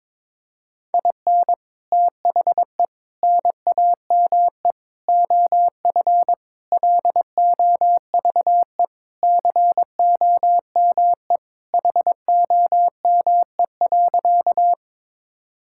Morse code, heard as a keyed tone, about 22 wpm, 700 Hz. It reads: IN THE NAME OF LOVE COME HOME.
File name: practice / 22wpm / Sentences with 3-Word Phrases - 22wpm